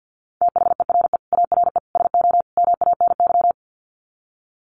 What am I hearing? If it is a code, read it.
Morse code, heard as a keyed tone, about 50 wpm, 700 Hz. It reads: THERE ARE SO MANY
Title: Sets of 4 Words 50wpm